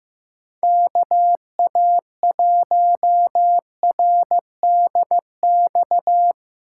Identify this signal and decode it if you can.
Morse code, heard as a keyed tone, about 17 wpm, 700 Hz. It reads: KA1RDX
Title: Speed Racing - USA 2x3 Call Signs 15wpm